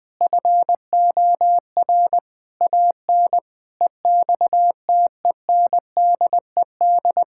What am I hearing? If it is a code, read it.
Morse code, heard as a keyed tone, about 20 wpm, 700 Hz. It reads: FOR AN EXTENDED
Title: Sentences with 6-Word Phrases - Spoken Beginning - 20wpm